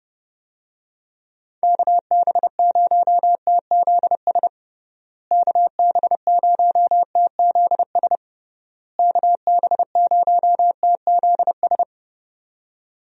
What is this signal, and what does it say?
Morse code, heard as a keyed tone, about 32 wpm, 700 Hz. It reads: KB0TZS KB0TZS KB0TZS
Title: USA Call Signs - 6 Characters - Repeated 3x - 30wpm